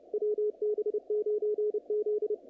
Morse code, heard as a keyed tone, about 28 wpm, 440 Hz, with radio noise. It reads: WB9Z
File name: contests / naqp-cw-k5tr-op-k5pi- -2200